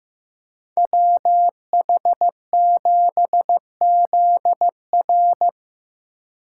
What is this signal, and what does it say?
Morse code, heard as a keyed tone, about 18 wpm, 700 Hz. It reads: WH7ZR
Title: Speed Racing - USA 2x2 Call Signs 15wpm